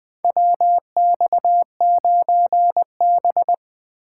Morse code, heard as a keyed tone, about 20 wpm, 700 Hz. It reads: WX9B
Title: USA Call Signs - 4 Characters - 4x Character Spacing - 20wpm